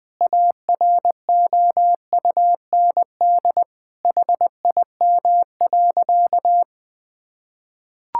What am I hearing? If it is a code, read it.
Morse code, heard as a keyed tone, about 20 wpm, 700 Hz. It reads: AROUND HIM.
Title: Sentences from Top 1000 Words - Review Part 6 - 20wpm